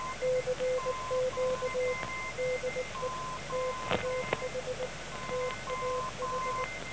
Morse code, heard as a keyed tone, about 19 wpm, 510 Hz, with noise. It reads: CQ DE TAH TAH
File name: TAH 17008.5 0900